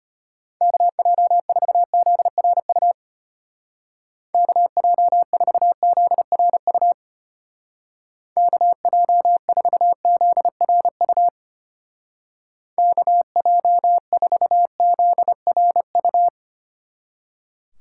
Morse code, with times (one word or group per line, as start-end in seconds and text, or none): 0.6-2.9 KJ4ZRU
4.3-6.9 KJ4ZRU
8.4-11.3 KJ4ZRU
12.8-16.3 KJ4ZRU